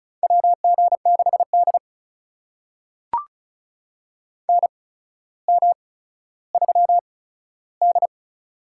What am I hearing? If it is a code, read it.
Morse code, heard as a keyed tone, about 36 wpm, spaced as for 15 wpm, 700 Hz. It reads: WG6D NM3D
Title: USA Call Signs - 4 Characters - 8x Character Spacing - 35wpm